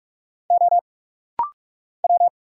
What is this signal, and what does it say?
Morse code, heard as a keyed tone, about 44 wpm, 700 Hz. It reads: O W
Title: Single Letter-Number Alt - Rapid-Fire 45wpm